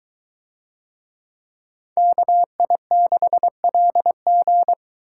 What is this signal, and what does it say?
Morse code, heard as a keyed tone, about 22 wpm, 700 Hz. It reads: KI6LG